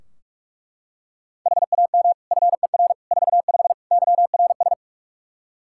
Morse code, heard as a keyed tone, about 41 wpm, 700 Hz. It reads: HAM FER 45 YRS
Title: K1USN SST QSO Practice 45wpm